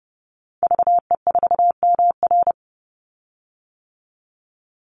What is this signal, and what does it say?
Morse code, heard as a keyed tone, about 30 wpm, 700 Hz. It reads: VE4MR
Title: Call Signs - 30wpm